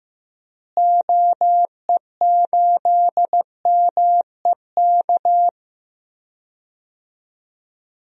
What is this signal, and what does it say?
Morse code, heard as a keyed tone, about 15 wpm, 700 Hz. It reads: OE8MEK
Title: DXCC-100 Call Signs - 6 Characters - Repeated 2x - 15wpm